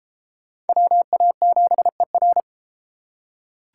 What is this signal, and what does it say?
Morse code, heard as a keyed tone, about 33 wpm, 700 Hz. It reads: WA7ER